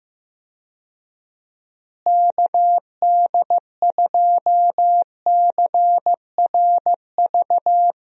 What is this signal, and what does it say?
Morse code, heard as a keyed tone, about 15 wpm, 700 Hz. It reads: KD2CRV
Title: USA Call Signs - 6 Characters - Repeated 2x - 15wpm